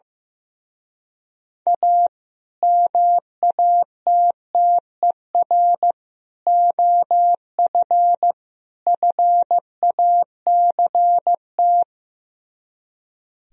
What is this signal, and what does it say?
Morse code, heard as a keyed tone, about 15 wpm, 700 Hz. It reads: A MATTER OF FACT
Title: Speed-Racing - Sets of 4 Words 15wpm